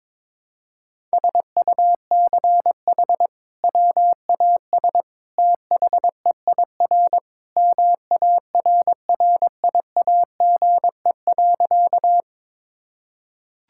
Morse code, heard as a keyed tone, about 22 wpm, 700 Hz. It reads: SUCH WAS THEIR MARRIAGE.
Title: Sentences from Top 1000 Words - Review Part 5 - 22wpm